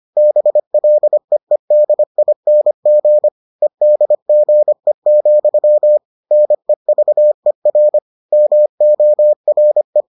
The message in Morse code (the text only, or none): BLEEDING EDGE, NEVER MORE